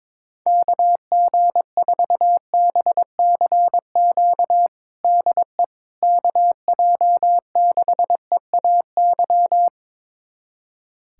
Morse code, heard as a keyed tone, about 21 wpm, 700 Hz. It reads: KG4BCQ DE KJ6EAY
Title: USA Callsign DE Callsign - 6 Characters - 22wpm